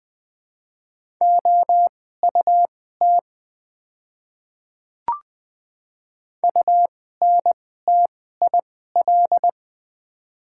Morse code, heard as a keyed tone, about 20 wpm, spaced as for 14 wpm, 700 Hz. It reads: OUT UNTIL
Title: 2x Character Spacing - Top 300 Words - 20wpm